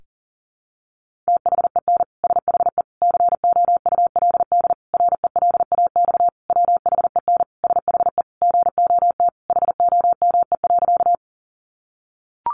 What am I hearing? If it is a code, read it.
Morse code, heard as a keyed tone, about 40 wpm, 700 Hz. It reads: THEN SHE COULD RELAX WHEN SHE GOT HOME.
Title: Sentences from Top 1400 Words 40wpm